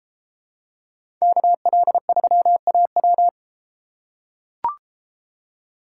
Morse code, heard as a keyed tone, about 33 wpm, 700 Hz. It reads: KL3AW